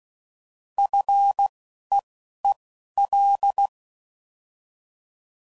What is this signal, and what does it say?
Morse code, heard as a keyed tone, about 16 wpm, spaced as for 11 wpm, 800 Hz. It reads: FEEL
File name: Basic Sets / QST de WA7RSO / Combnations